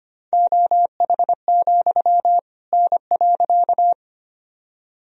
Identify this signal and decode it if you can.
Morse code, heard as a keyed tone, about 25 wpm, 700 Hz. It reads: OH, N.
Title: Sentences from Top 500 Words and Abbreviations 25wpm